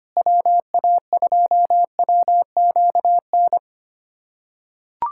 Morse code, heard as a keyed tone, about 25 wpm, 700 Hz. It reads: WA2WQN
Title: USA Callsign DE Callsign 25wpm